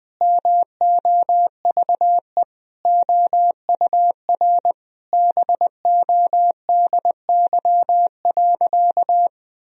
Morse code, heard as a keyed tone, about 21 wpm, 700 Hz. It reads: MOVE OUR BODY.
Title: Sentences from Top 1000 Words - Review Part 2 - 20wpm